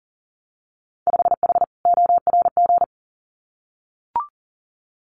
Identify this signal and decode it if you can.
Morse code, heard as a keyed tone, about 42 wpm, 700 Hz. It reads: 5H ORG
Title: Winter Field Day Exchanges Encore 40wpm